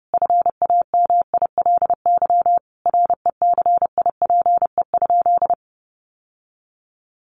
Morse code, heard as a keyed tone, about 30 wpm, 700 Hz. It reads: FAMILY RECIPE?
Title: Sentences from Top 1200 Words 30wpm